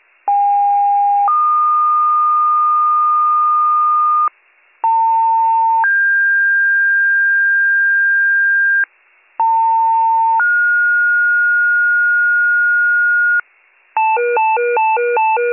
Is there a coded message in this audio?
Yes: two-tone page from 0.3 to 13.4 s, heard as a one-second tone, then a three-second tone, about 810 then 1220 Hz, about 900 then 1600 Hz, about 900 then 1430 Hz, with noise.